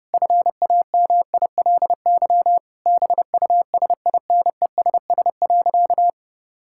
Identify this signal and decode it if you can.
Morse code, heard as a keyed tone, about 30 wpm, 700 Hz. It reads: FAMILY BUSINESS.
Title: Sentences from Top 1000 Words - Review Part 5 - 30wpm